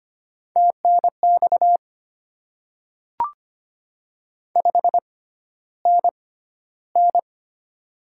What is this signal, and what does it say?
Morse code, heard as a keyed tone, about 25 wpm, spaced as for 13 wpm, 700 Hz. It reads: TNX 5NN
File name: Top 20 QSO Elements - 6x Character Spacing - 25wpm